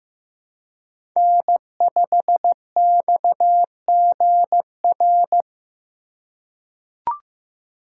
Morse code, heard as a keyed tone, about 15 wpm, 700 Hz. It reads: N5XGR
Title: USA Call Signs - 3x Character Spacing - 15wpm